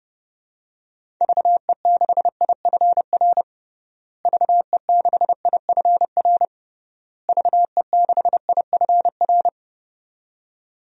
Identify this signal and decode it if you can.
Morse code, heard as a keyed tone, about 29 wpm, 700 Hz. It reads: VE6IFR VE6IFR VE6IFR
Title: DXCC-100 Call Signs - 6 Characters - Repeated 3x - 30wpm